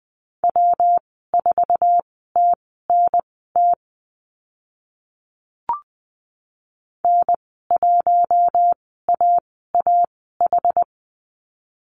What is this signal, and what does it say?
Morse code, heard as a keyed tone, about 20 wpm, spaced as for 14 wpm, 700 Hz. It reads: W4TNT N1AAH
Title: 161 Lesson 26g - Letter g - Call Signs 20wpm